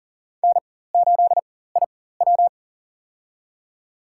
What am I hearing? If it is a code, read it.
Morse code, heard as a keyed tone, about 40 wpm, spaced as for 19 wpm, 700 Hz. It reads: N8IW
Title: USA Call Signs - 4 Characters - 4x Character Spacing - 40wpm